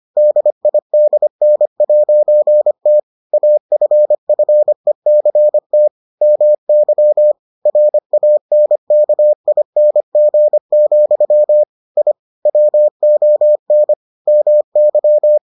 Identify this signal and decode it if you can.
Morse code, heard as a keyed tone, about 25 wpm, 600 Hz. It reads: DIDN'T AFFECT MY RANKING, I WON MY